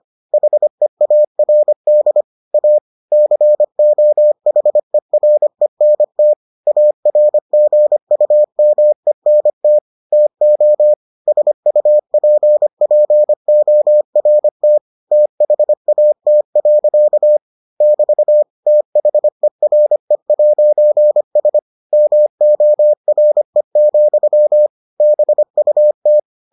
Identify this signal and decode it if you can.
Morse code, heard as a keyed tone, about 25 wpm, 600 Hz. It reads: HEARD A COHERENT ARGUMENT TO SUPPORT THAT. <BT> THERE'S MORE, BUT